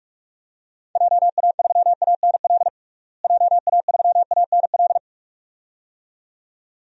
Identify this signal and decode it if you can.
Morse code, heard as a keyed tone, about 46 wpm, 700 Hz. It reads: JA3ANL JA3ANL
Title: DXCC-100 Call Signs - 6 Characters - Repeated 2x - 45wpm